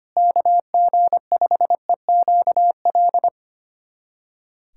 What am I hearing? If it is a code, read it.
Morse code, heard as a keyed tone, about 25 wpm, 700 Hz. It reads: KG5EQL
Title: USA Callsign DE Callsign - 6 Characters - 25wpm